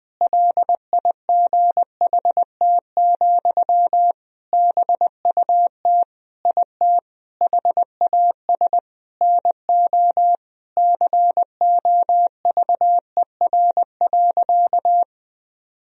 Morse code, heard as a keyed tone, about 20 wpm, 700 Hz. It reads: LIGHT, BUT IT HAS NO COVER.
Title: Sentences from Top 1000 Words - Review Part 1 - 20wpm